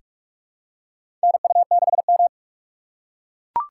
Morse code, heard as a keyed tone, about 42 wpm, 700 Hz. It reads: NU6M
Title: USA Call Signs - 2x Character Spacing - 45wpm